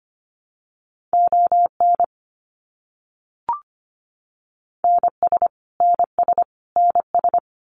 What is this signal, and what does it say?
Morse code, heard as a keyed tone, about 25 wpm, 700 Hz. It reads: ON NS NS NS
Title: Canadian Provinces and Territories - Repeated 3x - 25wpm